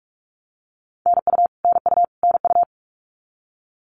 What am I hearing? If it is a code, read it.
Morse code, heard as a keyed tone, about 44 wpm, 700 Hz. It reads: NU NU NU
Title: Canadian Provinces and Territories - Repeated 3x - 45wpm